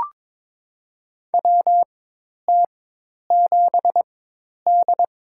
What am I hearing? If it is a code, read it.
Morse code, heard as a keyed tone, about 22 wpm, spaced as for 10 wpm, 700 Hz. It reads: WT7D
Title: USA Call Signs - 4 Characters - 4x Character Spacing - 22wpm